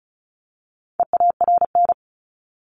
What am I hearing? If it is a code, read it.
Morse code, heard as a keyed tone, about 34 wpm, 700 Hz. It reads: EARN
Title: Top 1251-1500 Words 35wpm